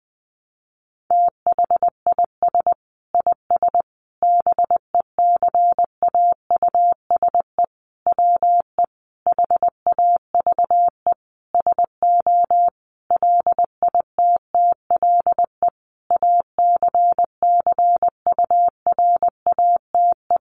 Morse code, heard as a keyed tone, about 20 wpm, 700 Hz. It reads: THIS IS BECAUSE WE HAVE SO LITTLE ACCURATE
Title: Sentences from Top 1200 Words 20wpm